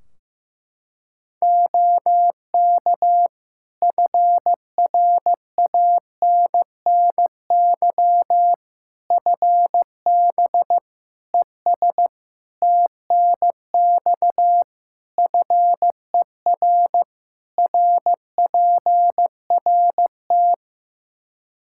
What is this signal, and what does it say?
Morse code, heard as a keyed tone, about 15 wpm, 700 Hz. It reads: OK FRANNY FB ES TNX FER RPRT
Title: Practice QSO - Exchange 2 - Info - 15wpm